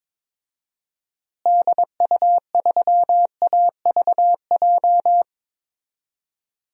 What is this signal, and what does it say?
Morse code, heard as a keyed tone, about 22 wpm, 700 Hz. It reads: DU3AVJ
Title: DXCC-100 Call Signs - 6 Characters - Repeated 2x - 22wpm